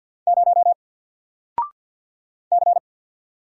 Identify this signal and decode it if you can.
Morse code, heard as a keyed tone, about 50 wpm, 700 Hz. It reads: 0 C
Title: Single Letter-Number 50wpm